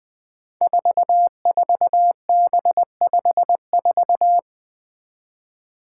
Morse code, heard as a keyed tone, about 21 wpm, 700 Hz. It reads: 44B54
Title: Five Challenging Letter-Number Combinations 20wpm